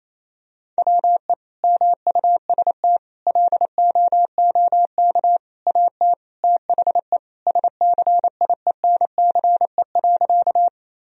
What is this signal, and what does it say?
Morse code, heard as a keyed tone, about 28 wpm, 700 Hz. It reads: WE MUST LOOK AT THE SCIENCE.
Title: Sentences from Top 1000 Words - Review Part 2 - 28wpm